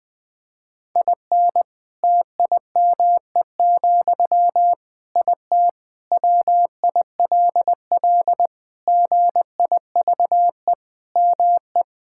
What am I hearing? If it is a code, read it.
Morse code, heard as a keyed tone, about 21 wpm, 700 Hz. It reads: IN TIME, IT WILL GIVE ME
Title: Sentences from Top 1600 Words 20wpm